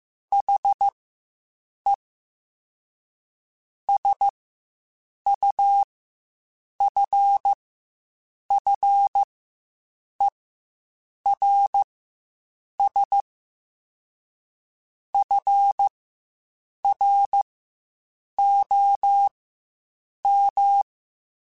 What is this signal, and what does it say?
Morse code, heard as a keyed tone, about 15 wpm, spaced as for 7 wpm, 800 Hz. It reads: HE SUFFERS FROM
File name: Quotes Spaced / Quotes Spaced-mp3s / HoofInMouth sp @10